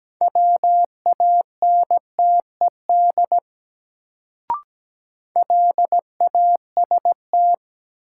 Morse code, heard as a keyed tone, about 17 wpm, 700 Hz. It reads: WANTED LAST
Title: Sets of 2 Words - New Word Drill - 17wpm